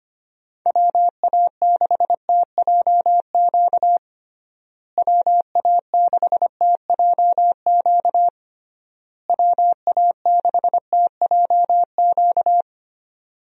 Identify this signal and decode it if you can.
Morse code, heard as a keyed tone, about 25 wpm, 700 Hz. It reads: WA6TJQ WA6TJQ WA6TJQ
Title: USA Call Signs - 6 Characters - Repeated 3x - 25wpm